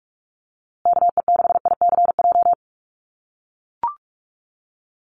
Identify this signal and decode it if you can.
Morse code, heard as a keyed tone, about 44 wpm, 700 Hz. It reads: KE6ICJ
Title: USA Call Signs Repeated 2x 45wpm